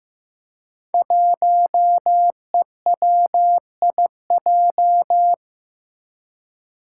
Morse code, heard as a keyed tone, about 15 wpm, 700 Hz. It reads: 1EWIJ
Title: Five Challenging Letter-Number Combinations 15wpm